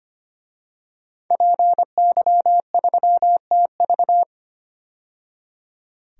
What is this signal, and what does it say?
Morse code, heard as a keyed tone, about 25 wpm, 700 Hz. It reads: PY3TV
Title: DXCC-100 Call Signs - 5 Characters - 25wpm